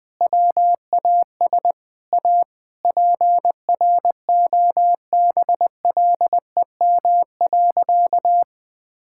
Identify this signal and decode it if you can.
Morse code, heard as a keyed tone, about 20 wpm, 700 Hz. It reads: WAS A PROBLEM.
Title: Sentences with 4-Word Phrases - 20wpm